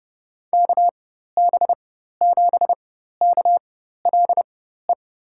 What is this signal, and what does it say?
Morse code, heard as a keyed tone, about 30 wpm, spaced as for 14 wpm, 700 Hz. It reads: KB7KLE
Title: USA Call Signs - 6 Characters - 4x Character Spacing - 30wpm